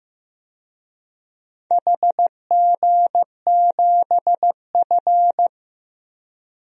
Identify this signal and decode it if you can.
Morse code, heard as a keyed tone, about 15 wpm, 700 Hz. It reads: HG7F
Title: DXCC-100 Call Signs - 4 Characters - 15wpm